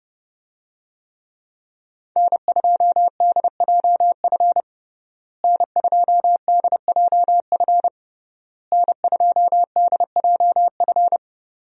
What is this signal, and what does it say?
Morse code, heard as a keyed tone, about 29 wpm, 700 Hz. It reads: N2DJF N2DJF N2DJF
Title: USA Call Signs - 5 Characters - Repeated 3x - 30wpm